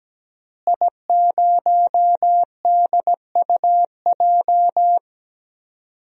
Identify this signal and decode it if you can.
Morse code, heard as a keyed tone, about 17 wpm, 700 Hz. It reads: I0DUJ
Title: DXCC-100 Call Signs - 5 Characters - Repeated 2x - 17wpm